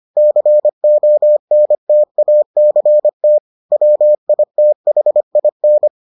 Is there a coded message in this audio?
Yes: Morse code from 0.2 to 5.9 s, heard as a keyed tone, about 25 wpm, 600 Hz. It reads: CONTACT WITHIN